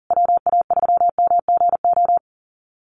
Morse code, heard as a keyed tone, about 40 wpm, 700 Hz. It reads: WA3MGO
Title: USA Call Signs Repeated 3x 40wpm